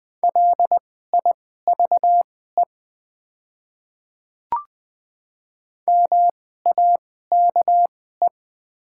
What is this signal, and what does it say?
Morse code, heard as a keyed tone, about 20 wpm, spaced as for 14 wpm, 700 Hz. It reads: LIVE MAKE
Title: 2x Character Spacing - Top 300 Words - 20wpm